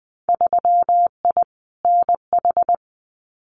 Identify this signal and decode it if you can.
Morse code, heard as a keyed tone, about 20 wpm, 700 Hz. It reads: 3I NH
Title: Winter Field Day Exchanges Encore 20wpm